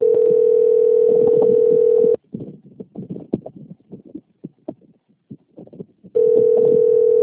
Ringback tone (North America), 440 + 480 Hz.
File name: out- - - - - .78622